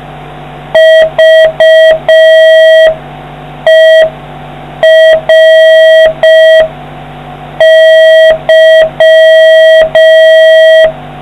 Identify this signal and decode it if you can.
Morse code, heard as a keyed tone, about 5 wpm, 1950 Hz, with noise. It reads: VERY